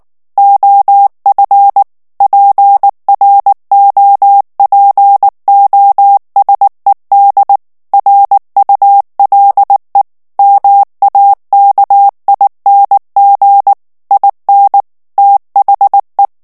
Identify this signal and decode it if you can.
Morse code, heard as a keyed tone, about 19 wpm, 800 Hz. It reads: OF PROPOSED RULE MAKING IN THE
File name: ANGLAIS texte clair 3 vitesse 18 mots minute